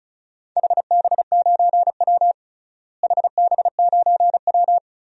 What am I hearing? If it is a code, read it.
Morse code, heard as a keyed tone, about 34 wpm, 700 Hz. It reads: HB9W HB9W
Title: DXCC-100 Call Signs - 4 Characters - Repeated 2x - 35wpm